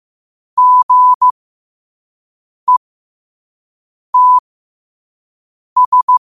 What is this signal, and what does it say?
Morse code, heard as a keyed tone, about 15 wpm, spaced as for 5 wpm, 1000 Hz. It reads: GETS